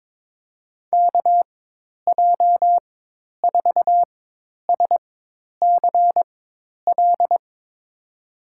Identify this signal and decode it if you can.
Morse code, heard as a keyed tone, about 22 wpm, spaced as for 10 wpm, 700 Hz. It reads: KJ4SCL
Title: USA Call Signs - 6 Characters - 4x Character Spacing - 22wpm